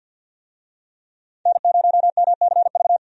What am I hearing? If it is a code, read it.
Morse code, heard as a keyed tone, about 51 wpm, 700 Hz. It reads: N0MCV
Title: USA Call Signs - 5 Characters - 50wpm